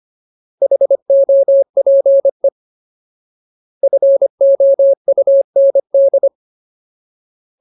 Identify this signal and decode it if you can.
Morse code, heard as a keyed tone, about 25 wpm, 550 Hz. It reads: HOPE FOUND
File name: Ultimate Word List / 25WPM Ultimate